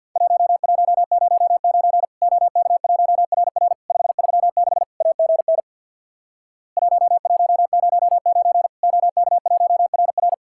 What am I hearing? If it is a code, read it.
Morse code, heard as a keyed tone, about 50 wpm, 700 Hz. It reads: 1109 OK1RR 536 AGN 1109 OK1RR
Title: Worked All Europe DX Contest - QTC Exchange with Repeats - 50wpm